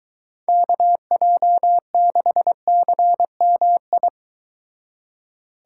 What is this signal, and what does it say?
Morse code, heard as a keyed tone, about 23 wpm, 700 Hz. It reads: KJ6CMI